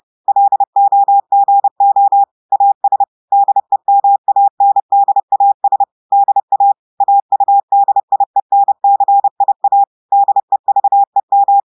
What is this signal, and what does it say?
Morse code, heard as a keyed tone, about 30 wpm, 800 Hz. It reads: LOGO AS DEMANDAS DA AUDIENCIA DEVEM
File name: audios / portugais / portugais texte en clair 2 vitesse 30 mots minutes